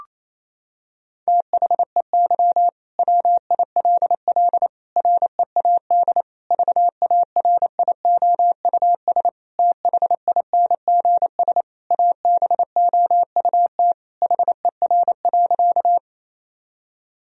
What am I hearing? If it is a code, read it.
Morse code, heard as a keyed tone, about 28 wpm, 700 Hz. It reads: THEY WILL READ VARIOUS THINGS ABOUT HER.
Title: Sentences from Top 1000 Words - Review Part 4 - 28wpm